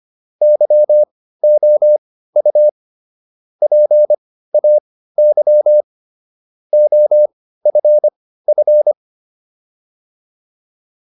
Morse code, heard as a keyed tone, about 25 wpm, spaced as for 15 wpm, 600 Hz. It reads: YOU PAY OFF